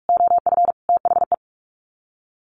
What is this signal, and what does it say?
Morse code, heard as a keyed tone, about 44 wpm, 700 Hz. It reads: OF THE
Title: Sets of 3 Words 45wpm